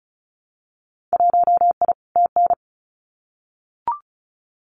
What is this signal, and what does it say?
Morse code, heard as a keyed tone, about 34 wpm, 690 Hz. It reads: 1I TN
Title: Winter Field Day Exchanges 35wpm